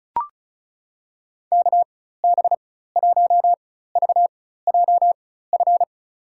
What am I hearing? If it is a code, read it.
Morse code, heard as a keyed tone, about 36 wpm, spaced as for 16 wpm, 700 Hz. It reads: KB1VJF
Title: USA Call Signs - 6 Characters - 4x Character Spacing - 35wpm